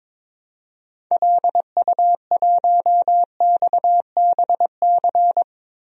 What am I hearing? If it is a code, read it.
Morse code, heard as a keyed tone, about 22 wpm, 700 Hz. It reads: LU1XBC
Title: DXCC-100 Call Signs - 6 Characters - Repeated 3x - 22wpm